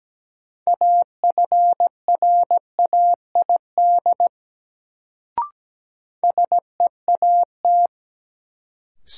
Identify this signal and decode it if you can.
Morse code, heard as a keyed tone, about 17 wpm, 700 Hz. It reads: AFRAID SEAT
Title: Sentences from Top 1500 Words - New Word Drill - 17wpm